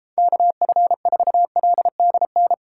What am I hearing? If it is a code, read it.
Morse code, heard as a keyed tone, about 31 wpm, 700 Hz. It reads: KF4LDN